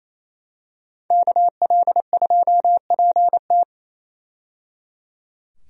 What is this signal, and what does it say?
Morse code, heard as a keyed tone, about 27 wpm, 700 Hz. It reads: KL2PT
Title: USA Call Signs - 5 Characters - 28wpm